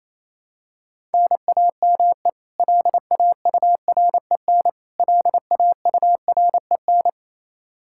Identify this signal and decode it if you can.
Morse code, heard as a keyed tone, about 28 wpm, 700 Hz. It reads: NAME LAUREN LAUREN
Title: Practice QSO - Exchange 1 - Beginning - 28wpm